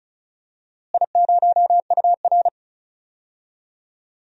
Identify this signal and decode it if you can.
Morse code, heard as a keyed tone, about 36 wpm, 700 Hz. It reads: I0UR
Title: DXCC-100 Call Signs - 4 Characters - Repeated 2x - 35wpm